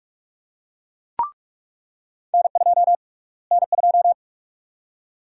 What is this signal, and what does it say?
Morse code, heard as a keyed tone, about 41 wpm, 700 Hz. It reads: N2 N2
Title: USA Call Sign Prefixes - Repeated 2x - 45wpm